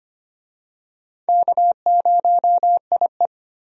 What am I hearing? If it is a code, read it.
Morse code, heard as a keyed tone, about 25 wpm, 700 Hz. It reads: K0IE
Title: USA Call Signs - 4 Characters - 25wpm